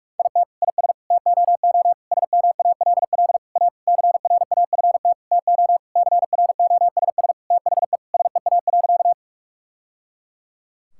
Morse code, heard as a keyed tone, about 46 wpm, 700 Hz. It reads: IT IS TOO SMALL A CRAFT TO CROSS THE SEA.
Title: Sentences from Top 1000 Words - Review Part 6 - 45wpm